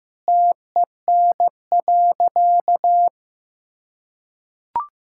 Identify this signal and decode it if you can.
Morse code, heard as a keyed tone, about 15 wpm, 700 Hz. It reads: TEN.
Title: Sentences from Top 1000 Words - Review Part 5 - 15wpm